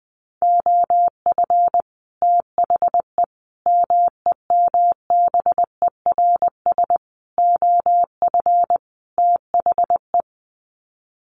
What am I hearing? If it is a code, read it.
Morse code, heard as a keyed tone, about 20 wpm, 700 Hz. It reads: OF THE MEMBERS OF THE
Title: Sets of 5 Words Encore2 20wpm